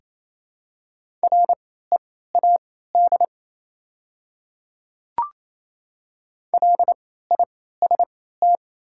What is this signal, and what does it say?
Morse code, heard as a keyed tone, about 28 wpm, spaced as for 16 wpm, 700 Hz. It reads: READ LIST
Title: 3x Character Spacing - Top 500 Words - 28wpm